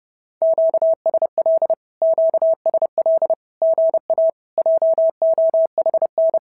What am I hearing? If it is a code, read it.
Morse code, heard as a keyed tone, about 30 wpm, 650 Hz. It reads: QSL QSL GA JOHN
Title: SOTA QSO Practice 30wpm